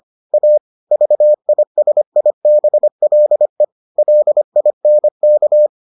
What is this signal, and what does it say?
Morse code, heard as a keyed tone, about 25 wpm, 600 Hz. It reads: A VISIBLE LINK